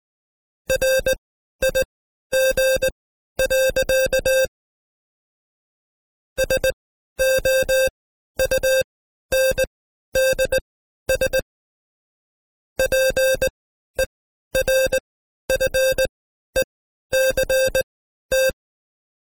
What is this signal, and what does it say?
Morse code, heard as a keyed tone, about 19 wpm, spaced as for 12 wpm, 530 Hz. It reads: RIG. SOUNDS PERFECT
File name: Code Quick / AudioDownld2 / Seg132-Stepper5 11WPM